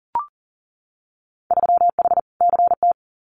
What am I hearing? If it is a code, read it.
Morse code, heard as a keyed tone, about 40 wpm, 700 Hz. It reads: 3H CT
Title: Winter Field Day Exchanges Encore 40wpm